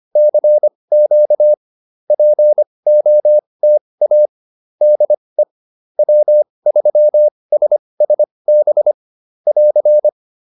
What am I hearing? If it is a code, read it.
Morse code, heard as a keyed tone, about 25 wpm, spaced as for 20 wpm, 600 Hz. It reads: CQ POTA DE W3SSB <AR>
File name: POTA210a 20